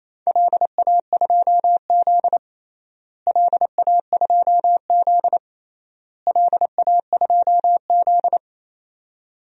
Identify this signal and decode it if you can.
Morse code, heard as a keyed tone, about 28 wpm, 700 Hz. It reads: LA2Z LA2Z LA2Z
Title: DXCC-100 Call Signs - 4 Characters - Repeated 3x - 28wpm